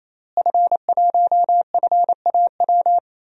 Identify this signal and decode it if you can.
Morse code, heard as a keyed tone, about 28 wpm, 690 Hz. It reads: F1FAW
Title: DXCC-100 Call Signs - 5 Characters - Repeated 3x - 28wpm